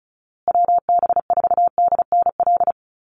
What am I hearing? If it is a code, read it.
Morse code, heard as a keyed tone, about 34 wpm, 700 Hz. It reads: WB4DNL
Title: Call Signs Repeated 2x - 35wpm